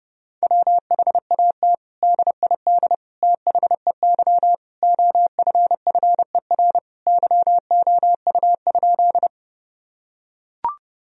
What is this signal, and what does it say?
Morse code, heard as a keyed tone, about 32 wpm, 700 Hz. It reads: WHAT DID THEY OFFER YOU?
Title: Sentences from Top 1000 Words - Review Part 4 - 30wpm